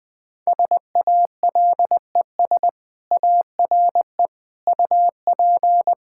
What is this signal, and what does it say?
Morse code, heard as a keyed tone, about 20 wpm, 700 Hz. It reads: SALES ARE UP
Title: Sentences with 7-Word Phrases - 20wpm